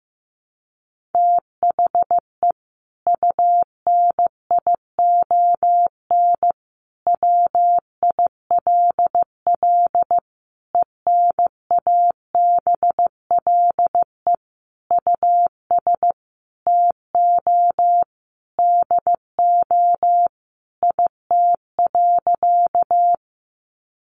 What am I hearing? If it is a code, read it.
Morse code, heard as a keyed tone, about 15 wpm, 700 Hz. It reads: THE UNION WILL ENABLE US TO DO IT.
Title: Sentences from Top 1500 Words 15wpm